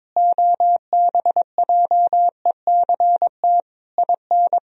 Morse code, heard as a keyed tone, about 22 wpm, 700 Hz. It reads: OBJECT IN
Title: Sentences from Top 1000 Words - Review Part 3 - 22wpm